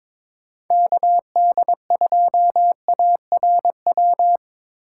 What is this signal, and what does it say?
Morse code, heard as a keyed tone, about 25 wpm, 700 Hz. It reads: KD2ARW
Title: Speed Racing - USA 2x3 Call Signs 22wpm